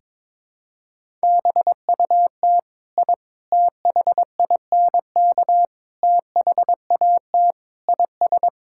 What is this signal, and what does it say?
Morse code, heard as a keyed tone, about 22 wpm, 700 Hz. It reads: BUT I THINK THAT IS
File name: practice / 22wpm / Sentences with 4-Word Phrases - 22wpm